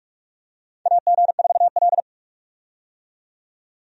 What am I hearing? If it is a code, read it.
Morse code, heard as a keyed tone, about 41 wpm, 700 Hz. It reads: AG4L
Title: USA Call Signs - 4 Characters - 45wpm